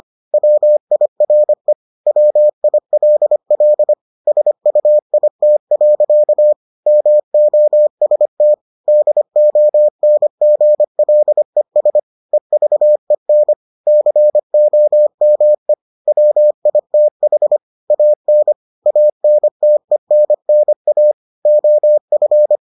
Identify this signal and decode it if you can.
Morse code, heard as a keyed tone, about 25 wpm, 600 Hz. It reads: WIRE WILL SUIT. MOST DONGLES EVEN COME WITH AN ANTENNA OF